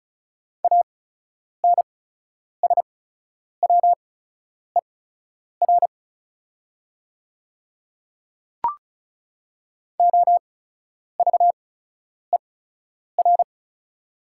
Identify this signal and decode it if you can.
Morse code, heard as a keyed tone, about 34 wpm, spaced as for 10 wpm, 700 Hz. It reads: ANSWER OVER
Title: 8x Character Spacing - Top 300 Words - 35wpm